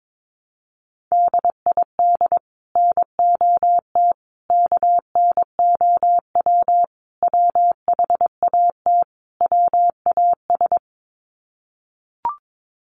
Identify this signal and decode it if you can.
Morse code, heard as a keyed tone, about 22 wpm, 700 Hz. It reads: DID NOT KNOW WHAT WAS
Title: Sets of 5 Words Encore2 22wpm